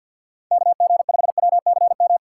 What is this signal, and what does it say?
Morse code, heard as a keyed tone, about 50 wpm, 700 Hz. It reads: KG5WCM
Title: USA Call Signs - 6 Characters - 6x Character Spacing - 50wpm